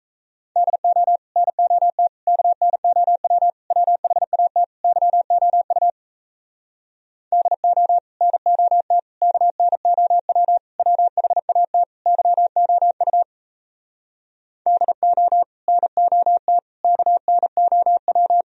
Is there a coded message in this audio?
Yes: Morse code from 0.6 to 18.4 s, heard as a keyed tone, about 37 wpm, 700 Hz. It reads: DO NOT KNOW WHAT YOU DO NOT KNOW WHAT YOU DO NOT KNOW